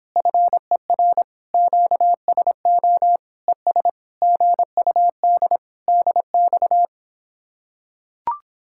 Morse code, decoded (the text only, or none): FER QSO ES GUD DX